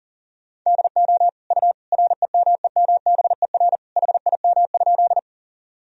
Morse code, heard as a keyed tone, about 40 wpm, 700 Hz. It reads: DO U REMEMBER HIM?
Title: Sentences from Top 400 Words and Abbreviations 40wpm